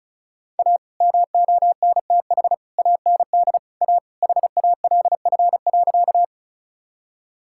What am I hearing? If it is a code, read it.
Morse code, heard as a keyed tone, about 34 wpm, 700 Hz. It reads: A MONTH AND A HALF.
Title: Sentences from Top 1000 Words - Review Part 5 - 35wpm